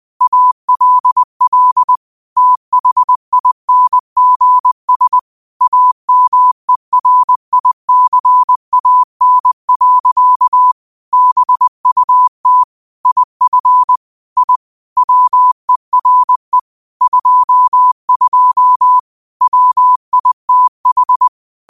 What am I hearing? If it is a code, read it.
Morse code, heard as a keyed tone, about 20 wpm, 1000 Hz. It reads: ALL THINGS AMERICAN. BUT IF I WERE 22 WITH
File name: qotdfiles / 20wpm